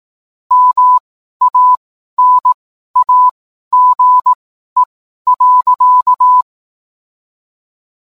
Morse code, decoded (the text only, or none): MANAGE.